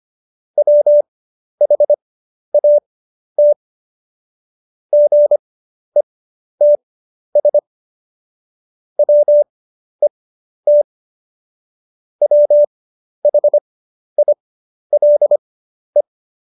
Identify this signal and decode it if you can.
Morse code, heard as a keyed tone, about 25 wpm, spaced as for 11 wpm, 590 Hz. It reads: WHAT GETS WET WHILE
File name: Jokes-25-char-10-eff